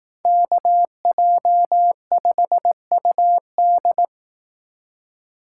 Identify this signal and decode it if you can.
Morse code, heard as a keyed tone, about 18 wpm, 700 Hz. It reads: KJ5UD
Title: Speed Racing - USA 2x2 Call Signs 15wpm